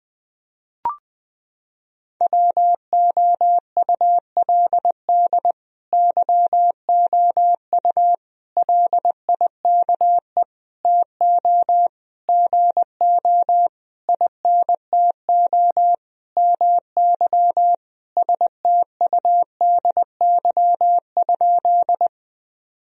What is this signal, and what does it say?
Morse code, heard as a keyed tone, about 20 wpm, 700 Hz. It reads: WOULD YOU LIKE TO GO INTO MY STUDY?
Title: Sentences from Top 1000 Words - Review Part 1 - 20wpm